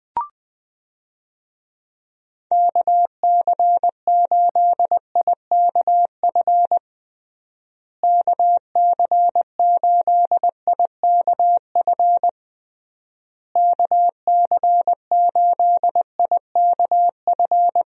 Morse code, heard as a keyed tone, about 20 wpm, 700 Hz. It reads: KC8IKF KC8IKF KC8IKF
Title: USA Call Signs - 6 Characters - Repeated 3x - 20wpm